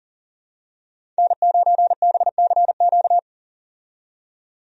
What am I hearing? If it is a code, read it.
Morse code, heard as a keyed tone, about 40 wpm, 700 Hz. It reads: N9BCQ
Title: USA Call Signs - 5 Characters - 40wpm